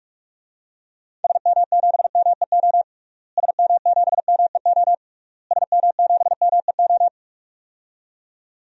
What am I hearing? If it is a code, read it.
Morse code, heard as a keyed tone, about 46 wpm, 700 Hz. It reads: SM7MEO SM7MEO SM7MEO
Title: DXCC-100 Call Signs - 6 Characters - Repeated 3x - 45wpm